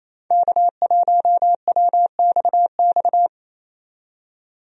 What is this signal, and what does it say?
Morse code, heard as a keyed tone, about 28 wpm, 700 Hz. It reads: K1WXX
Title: USA Call Signs - 5 Characters - 28wpm